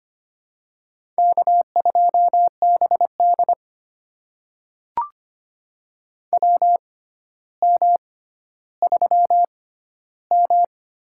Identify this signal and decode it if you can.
Morse code, heard as a keyed tone, about 25 wpm, spaced as for 13 wpm, 700 Hz. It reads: K2BD WM3M
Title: USA Call Signs - 4 Characters - 6x Character Spacing - 25wpm